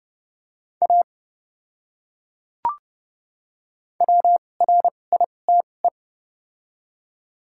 Morse code, heard as a keyed tone, about 30 wpm, spaced as for 22 wpm, 700 Hz. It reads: A WRITE
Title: 2x Character Spacing - Top 100 Words - 30wpm